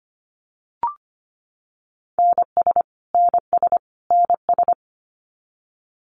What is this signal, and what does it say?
Morse code, heard as a keyed tone, about 25 wpm, 700 Hz. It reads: NS NS NS
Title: Canadian Provinces and Territories - Repeated 3x - 25wpm